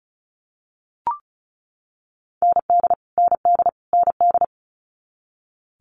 Morse code, heard as a keyed tone, about 36 wpm, 700 Hz. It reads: ND ND ND
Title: Multi-letter Phonemes Repeated 3x 35wpm